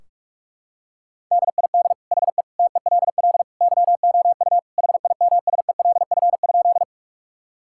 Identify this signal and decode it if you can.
Morse code, heard as a keyed tone, about 46 wpm, 700 Hz. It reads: DID HE TELL YOU HIMSELF?
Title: Sentences from Top 1600 Words 45wpm